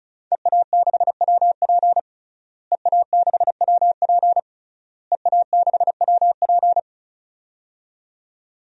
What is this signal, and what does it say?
Morse code, heard as a keyed tone, about 36 wpm, 700 Hz. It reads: EA6WP EA6WP EA6WP
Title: DXCC-100 Call Signs - 5 Characters - Repeated 3x - 35wpm